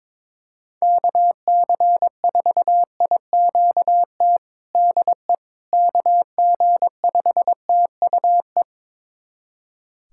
Morse code, heard as a keyed tone, about 22 wpm, 700 Hz. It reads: KC4IQT DE KG5TUE
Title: USA Callsign DE Callsign 22wpm